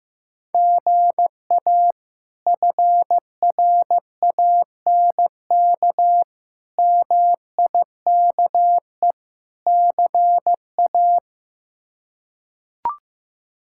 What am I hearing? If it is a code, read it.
Morse code, heard as a keyed tone, about 15 wpm, 700 Hz. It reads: GA FRANK MIKE CA
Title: K1USN SST QSO Practice - 15wpm